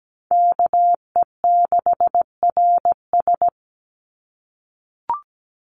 Morse code, heard as a keyed tone, about 17 wpm, 700 Hz. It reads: KE6RS
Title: NA SOTA Chaser Callsigns 17wpm